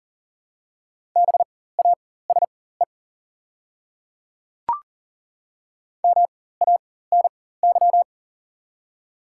Morse code, heard as a keyed tone, about 40 wpm, spaced as for 19 wpm, 700 Hz. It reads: BASE MANY
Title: 4x Character Spacing - Top 300 Words - 40wpm